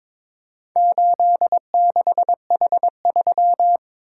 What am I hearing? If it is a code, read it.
Morse code, heard as a keyed tone, about 22 wpm, 700 Hz. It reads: 86H3
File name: practice / 22wpm / Four Letter-Number Combinations - 22wpm